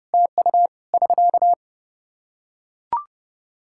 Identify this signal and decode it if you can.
Morse code, heard as a keyed tone, about 30 wpm, 700 Hz. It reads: TU <SK>
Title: Practice QSO - Exchange 4 - Ending - 30wpm